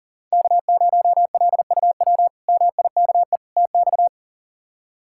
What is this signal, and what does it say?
Morse code, heard as a keyed tone, about 40 wpm, 690 Hz. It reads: K0LUW MIKE TX
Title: K1USN SST QSO Practice - 40wpm